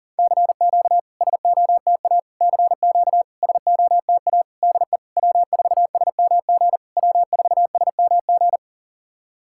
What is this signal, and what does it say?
Morse code, heard as a keyed tone, about 44 wpm, 700 Hz. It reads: CQ SOTA CQ SOTA DE W4SMG W4SMG
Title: SOTA QSO Practice 40wpm